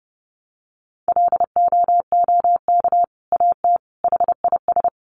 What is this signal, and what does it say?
Morse code, heard as a keyed tone, about 30 wpm, 700 Hz. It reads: LOOK AT HIS
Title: Sentences from Top 300 Words 30wpm